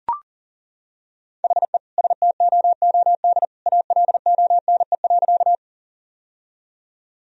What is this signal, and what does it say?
Morse code, heard as a keyed tone, about 37 wpm, 700 Hz. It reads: HE STOOD ALONE.
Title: Sentences from Top 1000 Words - Review Part 2 - 40wpm